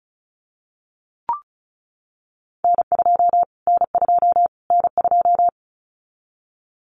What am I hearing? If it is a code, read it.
Morse code, heard as a keyed tone, about 34 wpm, 700 Hz. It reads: N2 N2 N2
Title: Callsign Prefixes for Top 25 DXCC - Repeated 3x - 35wpm